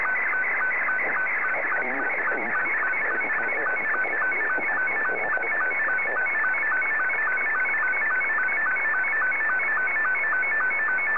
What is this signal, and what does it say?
SSTV (slow-scan television), Scottie 2, with speech.